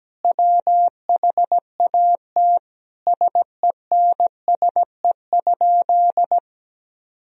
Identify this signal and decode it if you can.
Morse code, heard as a keyed tone, about 17 wpm, 700 Hz. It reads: WHAT SENSE?
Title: Sentences from Top 1000 Words - Review Part 4 - 17wpm